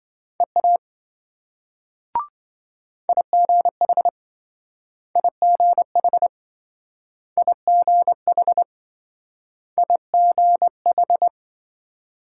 Morse code, heard as a keyed tone, about 26 wpm, 700 Hz. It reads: EA IGH IGH IGH IGH